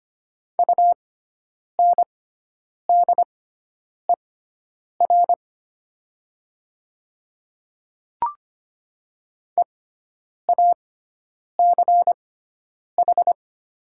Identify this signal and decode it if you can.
Morse code, heard as a keyed tone, about 25 wpm, spaced as for 9 wpm, 700 Hz. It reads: UNDER EACH
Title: 6x Character Spacing - Top 300 Words - 25wpm